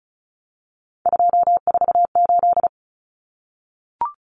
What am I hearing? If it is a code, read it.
Morse code, heard as a keyed tone, about 34 wpm, 700 Hz. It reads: 248
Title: Numbers - 3-Digit 35wpm